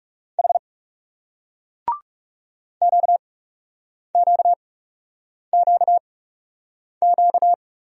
Morse code, heard as a keyed tone, about 41 wpm, spaced as for 8 wpm, 700 Hz. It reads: H QQQQ